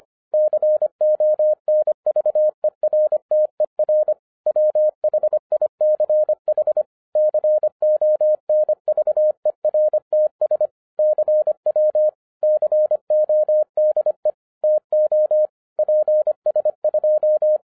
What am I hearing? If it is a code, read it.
Morse code, heard as a keyed tone, about 24 wpm, 610 Hz. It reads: CONVERTER WHICH CONVERTS CW CODE TO PS2
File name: 2011-10-mp3 / n09-CW TO PS2 CONVERTER 25wpm